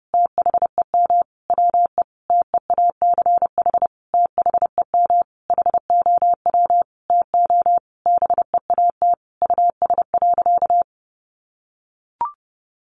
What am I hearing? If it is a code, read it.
Morse code, heard as a keyed tone, about 28 wpm, 700 Hz. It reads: THEM WE TEACH THEM HOW TO BEAT US.
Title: Sentences from Top 1300 Words 30wpm